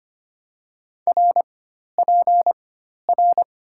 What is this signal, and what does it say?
Morse code, heard as a keyed tone, about 25 wpm, 700 Hz. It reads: R P R
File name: Top 90 QSO Elements - 4x Character Spacing - 25wpm